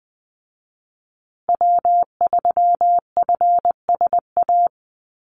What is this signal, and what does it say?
Morse code, heard as a keyed tone, about 20 wpm, 700 Hz. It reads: W3FSA
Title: Call Signs Repeated 2x - 20wpm